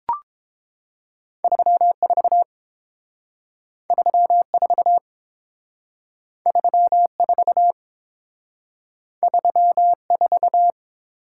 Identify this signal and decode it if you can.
Morse code, heard as a keyed tone, about 28 wpm, 700 Hz. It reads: 34 34 34 34